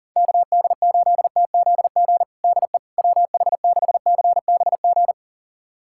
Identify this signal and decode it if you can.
Morse code, heard as a keyed tone, about 40 wpm, 700 Hz. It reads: KD8TZG DE WH6CBG
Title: USA Callsign DE Callsign - 6 Characters - 40wpm